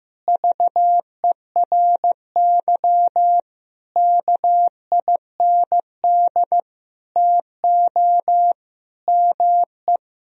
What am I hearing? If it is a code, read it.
Morse code, heard as a keyed tone, about 15 wpm, 700 Hz. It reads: VERY KIND TO ME
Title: Sentences from Top 1000 Words - Review Part 1 - 15wpm